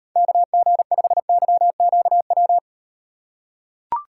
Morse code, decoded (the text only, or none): KG5YQW